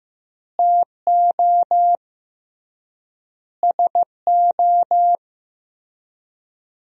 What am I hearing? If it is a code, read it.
Morse code, heard as a keyed tone, about 16 wpm, 700 Hz. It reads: TO SO
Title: 3x Word-Spacing - Sentences from Top 400 Words - 15wpm